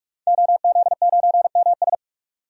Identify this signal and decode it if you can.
Morse code, heard as a keyed tone, about 44 wpm, 700 Hz. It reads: OZ9MS
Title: DXCC-100 Call Signs - 5 Characters - 45wpm